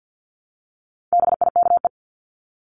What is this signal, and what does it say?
Morse code, heard as a keyed tone, about 48 wpm, 700 Hz. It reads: BIKE
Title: Top 1751-2000 Words 50wpm